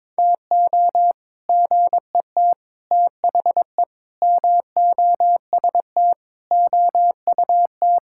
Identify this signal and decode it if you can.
Morse code, heard as a keyed tone, about 21 wpm, 700 Hz. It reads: TO GET THE MOST OUT
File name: practice / 22wpm / Sets of 7 Words - 22wpm